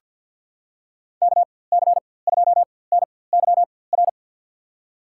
Morse code, heard as a keyed tone, about 51 wpm, spaced as for 23 wpm, 700 Hz. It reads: KC2NYR
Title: USA Call Signs - 6 Characters - 4x Character Spacing - 50wpm